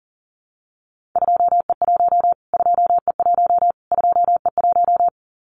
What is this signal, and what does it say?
Morse code, heard as a keyed tone, about 40 wpm, 700 Hz. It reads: 2E1 2E1 2E1
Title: Callsign Prefixes for Top 25 DXCC - Repeated 3x - 40wpm